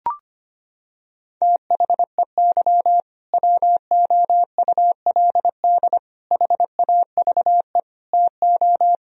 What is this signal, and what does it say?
Morse code, heard as a keyed tone, about 25 wpm, 700 Hz. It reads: THEY WOULD HAVE TO
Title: Sentences from Top 1000 Words - Review Part 1 - 25wpm